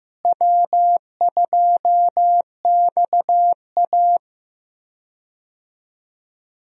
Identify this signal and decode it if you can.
Morse code, heard as a keyed tone, about 15 wpm, 700 Hz. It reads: W2XA
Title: USA Call Signs - 4 Characters - Repeated 2x - 15wpm